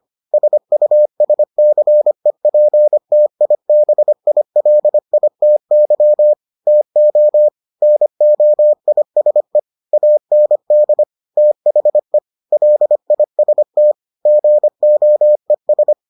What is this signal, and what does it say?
Morse code, heard as a keyed tone, about 25 wpm, 600 Hz. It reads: SUSCEPTIBILITY TO NOISE AND THE LIST GOES